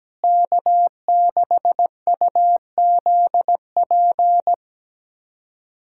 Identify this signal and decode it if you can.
Morse code, heard as a keyed tone, about 17 wpm, 700 Hz. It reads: K6UZP
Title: USA Callsign DE Callsign - 5 Characters - 17wpm